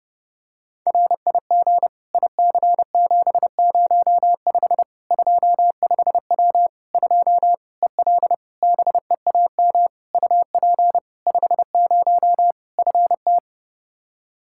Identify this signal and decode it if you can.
Morse code, heard as a keyed tone, about 30 wpm, 700 Hz. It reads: RIG IC705 25W 2 EL BEAM UP 50 FT
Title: Practice QSO - Exchange 2 - Info - 30wpm